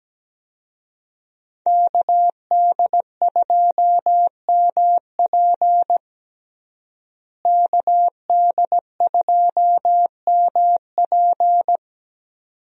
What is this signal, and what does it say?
Morse code, heard as a keyed tone, about 17 wpm, 700 Hz. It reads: KD2MP KD2MP
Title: USA Call Signs - 5 Characters - Repeated 2x - 17wpm